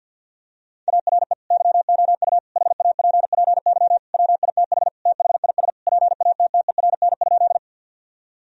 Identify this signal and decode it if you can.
Morse code, heard as a keyed tone, about 50 wpm, 700 Hz. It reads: ARE YOU HAPPY WITH THIS PATTERN?
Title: Sentences from Top 1000 Words - Review Part 2 - 50wpm